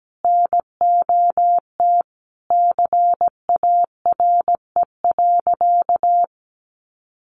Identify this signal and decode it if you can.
Morse code, heard as a keyed tone, about 17 wpm, 700 Hz. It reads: NOT CARE.
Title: Additional Easy Phrases 17wpm